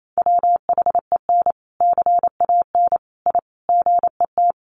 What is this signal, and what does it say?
Morse code, heard as a keyed tone, about 28 wpm, 700 Hz. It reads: WHEN CAN I GET
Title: Sentences from Top 1100 Words 28wpm